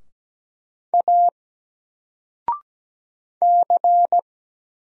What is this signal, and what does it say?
Morse code, heard as a keyed tone, about 17 wpm, 700 Hz. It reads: A C
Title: Single Character (Letter:Number:Punctuation:Prosign) 17wpm